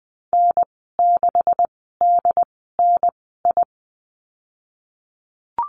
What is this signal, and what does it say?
Morse code, heard as a keyed tone, about 20 wpm, spaced as for 15 wpm, 700 Hz. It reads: N6DNI
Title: 167 Lesson 27g - Number 7 - Call Signs 20wpm